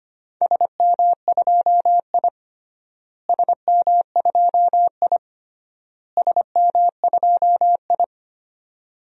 Morse code, heard as a keyed tone, about 25 wpm, 700 Hz. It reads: SM2I SM2I SM2I
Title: DXCC-100 Call Signs - 4 Characters - Repeated 3x - 25wpm